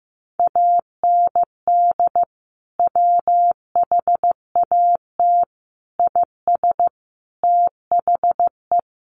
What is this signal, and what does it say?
Morse code, heard as a keyed tone, about 15 wpm, 700 Hz. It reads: AND WHAT IS THE
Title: Sentences from Top 800 Words 15wpm